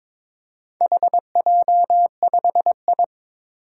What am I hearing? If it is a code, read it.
Morse code, heard as a keyed tone, about 22 wpm, 700 Hz. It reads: HJ5I
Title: Four Challenging Letter-Number Combinations 22wpm